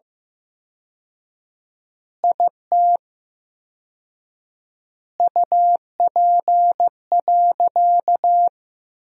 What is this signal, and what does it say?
Morse code, heard as a keyed tone, about 15 wpm, 700 Hz. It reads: IT UP.
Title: 4x Word-Spacing - Sentences from Top 100 Words - 15wpm